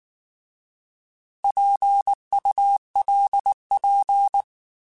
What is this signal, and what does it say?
Morse code, heard as a keyed tone, about 19 wpm, 800 Hz. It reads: PULP
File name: Alphabetical / Alphabetical mp3s / Ps @18